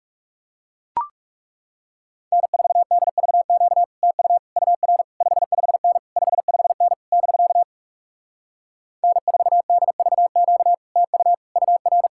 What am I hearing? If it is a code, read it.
Morse code, heard as a keyed tone, about 38 wpm, 700 Hz. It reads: N4DVQ TU UR 55N 55N <BK> N4DVQ TU UR
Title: Speed-Racing Common QSO Phrases - 30wpm